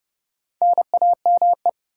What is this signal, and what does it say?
Morse code, heard as a keyed tone, about 40 wpm, spaced as for 35 wpm, 700 Hz. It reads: NAME
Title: Speed-Racing Common QSO Phrases - 30wpm